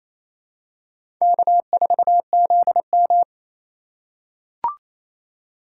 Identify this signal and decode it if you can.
Morse code, heard as a keyed tone, about 28 wpm, 700 Hz. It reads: K4ZM
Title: USA Call Signs - 4 Characters - 2x Character Spacing - 28wpm